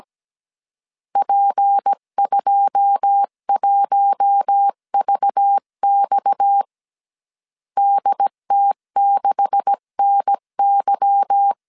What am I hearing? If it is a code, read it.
Morse code, heard as a keyed tone, about 17 wpm, 800 Hz. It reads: P21VX DT6NY
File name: 1500C S B 16wpm